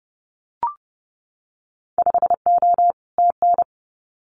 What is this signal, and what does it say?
Morse code, heard as a keyed tone, about 31 wpm, 700 Hz. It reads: 5O TN
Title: Winter Field Day Exchanges Encore 30wpm